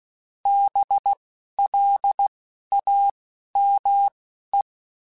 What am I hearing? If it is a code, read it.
Morse code, heard as a keyed tone, about 16 wpm, spaced as for 11 wpm, 800 Hz. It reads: BLAME
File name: Spaced / Chuckles Spaced-mp3s / HumanErr sp @15